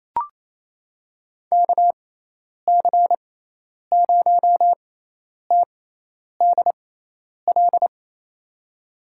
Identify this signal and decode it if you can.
Morse code, heard as a keyed tone, about 28 wpm, spaced as for 10 wpm, 700 Hz. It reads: KC0TDL
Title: USA Call Signs - 6 Characters - 6x Character Spacing - 28wpm